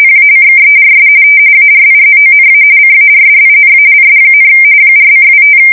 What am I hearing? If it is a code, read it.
RTTY, about 45.45 baud. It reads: PACKERS ARIZONA CARDINALS, 4:25
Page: 170 Shift 2125/2295